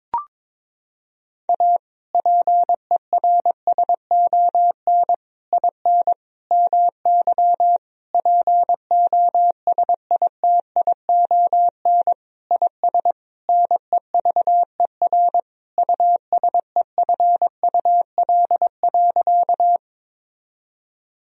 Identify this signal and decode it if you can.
Morse code, heard as a keyed tone, about 22 wpm, 700 Hz. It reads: A PERSON IN MY POSITION IS NEVER USEFUL.
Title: Sentences from Top 1000 Words - Review Part 4 - 22wpm